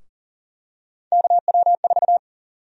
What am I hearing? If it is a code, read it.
Morse code, heard as a keyed tone, about 40 wpm, 700 Hz. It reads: KW4
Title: USA Call Sign Prefixes - Repeated 2x - 40wpm